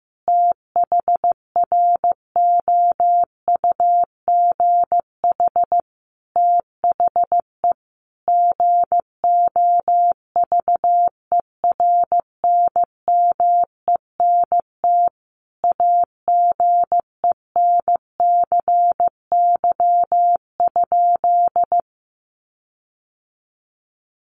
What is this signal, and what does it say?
Morse code, heard as a keyed tone, about 15 wpm, 700 Hz. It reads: THROUGH THE GOVERNMENT AGENCY?
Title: Sentences from Top 1300 Words 15wpm